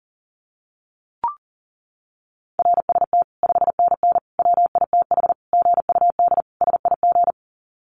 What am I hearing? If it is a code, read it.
Morse code, heard as a keyed tone, about 40 wpm, 700 Hz. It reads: RST 5NN WITH GUD SIG
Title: Common QSO Phrases 40wpm